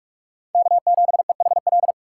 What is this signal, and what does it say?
Morse code, heard as a keyed tone, about 44 wpm, 700 Hz. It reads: K7EHL
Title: USA Call Signs - 5 Characters - 6x Character Spacing - 45wpm